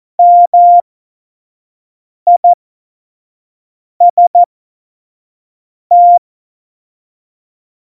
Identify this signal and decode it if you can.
Morse code, heard as a keyed tone, about 14 wpm, spaced as for 5 wpm, 700 Hz. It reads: MIST
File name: Morsekurs / Witze / Witz 19 Mai 2020, 25 F70